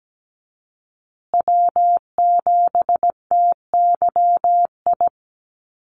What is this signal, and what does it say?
Morse code, heard as a keyed tone, about 17 wpm, 700 Hz. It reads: W7TYI
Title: USA Call Signs Repeated 2x 17wpm